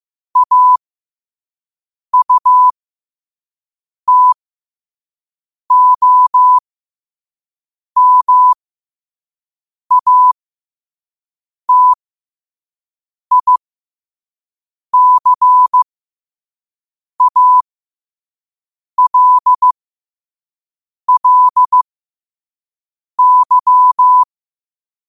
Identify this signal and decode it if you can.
Morse code, heard as a keyed tone, about 15 wpm, spaced as for 5 wpm, 1000 Hz. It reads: AUTOMATICALLY